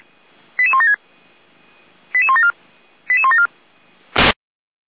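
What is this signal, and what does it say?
Five-tone selective call (ZVEI), heard as quick runs of five tones, tones of about 70 ms. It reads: 88176 88174 88174
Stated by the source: Five-tone ZVEI selective call received by SDR.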